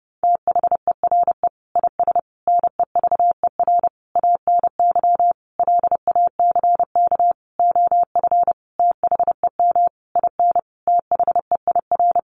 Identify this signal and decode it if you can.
Morse code, heard as a keyed tone, about 28 wpm, 700 Hz. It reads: THERE IS NEVER ANY LACK OF THEM IN THEIR
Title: Sentences from Top 1100 Words 30wpm